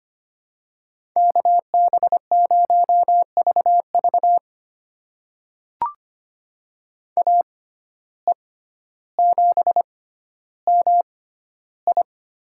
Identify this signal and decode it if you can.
Morse code, heard as a keyed tone, about 25 wpm, spaced as for 13 wpm, 700 Hz. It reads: KB0VV AE7MI
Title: USA Call Signs - 5 Characters - 6x Character Spacing - 25wpm